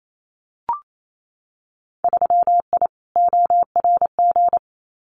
Morse code, heard as a keyed tone, about 28 wpm, 700 Hz. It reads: 3I ORG
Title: Winter Field Day Exchanges Encore 28wpm